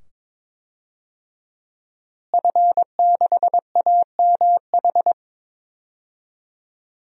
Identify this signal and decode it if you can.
Morse code, heard as a keyed tone, about 22 wpm, 700 Hz. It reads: F6AMH
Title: DXCC-100 Call Signs - 5 Characters - Repeated 2x - 22wpm